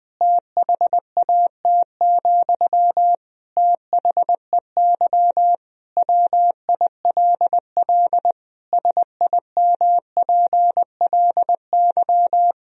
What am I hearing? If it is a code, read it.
Morse code, heard as a keyed tone, about 20 wpm, 700 Hz. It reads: THAT, THEY WILL SIMPLY
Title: Sentences from Top 1000 Words - Review Part 4 - 20wpm